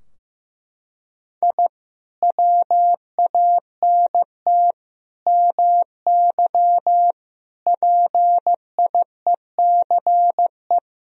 Morse code, heard as a keyed tone, about 15 wpm, 700 Hz. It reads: I WANT MY PIECE
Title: Sentences from Top 1000 Words - Review Part 2 - 15wpm